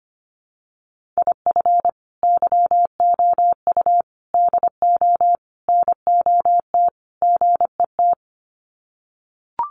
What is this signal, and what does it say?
Morse code, heard as a keyed tone, about 25 wpm, 700 Hz. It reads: IF YOU DO NOT GET
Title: Sets of 5 Words Encore2 25wpm